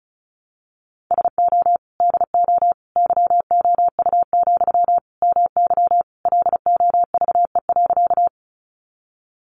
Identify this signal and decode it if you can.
Morse code, heard as a keyed tone, about 34 wpm, 700 Hz. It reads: SO DO YOU, MY LOVE.
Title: Sentences from Top 400 Words 35wpm